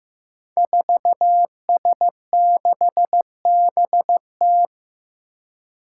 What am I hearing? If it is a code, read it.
Morse code, heard as a keyed tone, about 15 wpm, 700 Hz. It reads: 4S6BT
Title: Five Challenging Letter-Number Combinations 15wpm